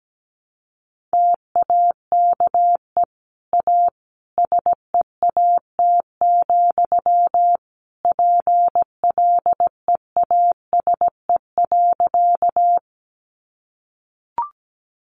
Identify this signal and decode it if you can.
Morse code, heard as a keyed tone, about 17 wpm, 700 Hz. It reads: TAKE A SEAT, PLEASE.
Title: Sentences from Top 1500 Words 17wpm